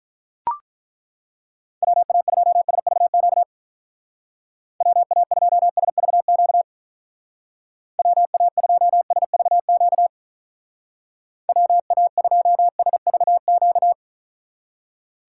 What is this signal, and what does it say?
Morse code, heard as a keyed tone, about 42 wpm, 700 Hz. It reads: WA2SVQ WA2SVQ WA2SVQ WA2SVQ
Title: Speed Racing - USA 2x3 Call Signs 35wpm